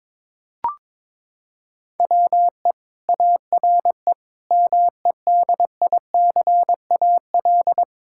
Morse code, heard as a keyed tone, about 22 wpm, 700 Hz. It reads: WE ARE MEDICAL
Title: Sentences from Top 1000 Words - Review Part 5 - 22wpm